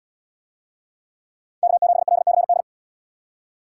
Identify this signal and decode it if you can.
Morse code, heard as a keyed tone, about 70 wpm, 700 Hz. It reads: K7LCD